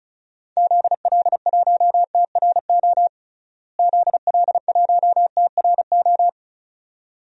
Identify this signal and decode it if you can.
Morse code, heard as a keyed tone, about 36 wpm, 700 Hz. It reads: ZL1TRO ZL1TRO
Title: DXCC-100 Call Signs - 6 Characters - Repeated 2x - 35wpm